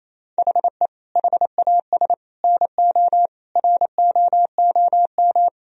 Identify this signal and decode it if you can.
Morse code, heard as a keyed tone, about 28 wpm, 700 Hz. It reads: HE HAS NO ROOM
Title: Sentences from Top 1000 Words - Review Part 4 - 28wpm